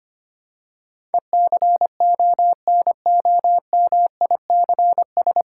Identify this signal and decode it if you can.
Morse code, heard as a keyed tone, about 25 wpm, 700 Hz. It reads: ECONOMICS
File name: Nine Letter Words - 25wpm